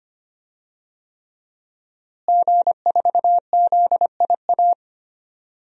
Morse code, heard as a keyed tone, about 25 wpm, 700 Hz. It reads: G4ZIA
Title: DXCC-100 Call Signs - 5 Characters - Repeated 3x - 25wpm